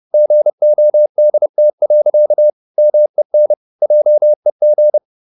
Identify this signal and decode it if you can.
Morse code, heard as a keyed tone, about 30 wpm, 600 Hz. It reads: GODT. MEN JEG
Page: Ferie 30 wpm